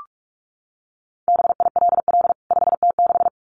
Morse code, heard as a keyed tone, about 44 wpm, 700 Hz. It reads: BILL 5T6
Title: CWT Exchange 45wpm